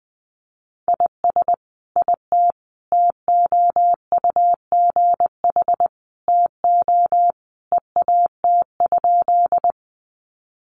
Morse code, heard as a keyed tone, about 20 wpm, 700 Hz. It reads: IS IT TOUGH TO EAT?
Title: Sentences from Top 1200 Words 20wpm